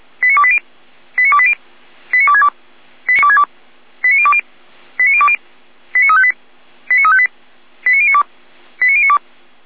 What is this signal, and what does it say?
Five-tone selective call (ZVEI), heard as quick runs of five tones, tones of about 70 ms. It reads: 89280 89280 89262 89020 89020 89358 89358 89092 89092